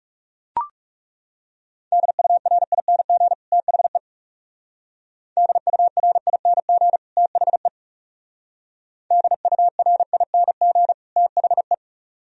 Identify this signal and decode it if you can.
Morse code, heard as a keyed tone, about 37 wpm, 700 Hz. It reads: DURING THE DURING THE DURING THE
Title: Speed-Racing - Sets of 2 Words 30wpm